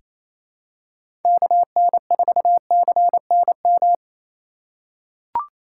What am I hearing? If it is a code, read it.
Morse code, heard as a keyed tone, about 28 wpm, 700 Hz. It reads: KN4CNM
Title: USA Call Signs - 6 Characters - 2x Character Spacing - 28wpm